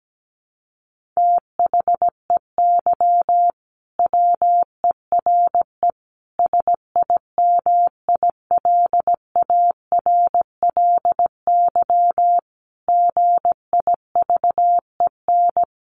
Morse code, heard as a keyed tone, about 17 wpm, 700 Hz. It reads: THEY WERE SIMILARLY GIVEN
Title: Sentences from Top 1400 Words 17wpm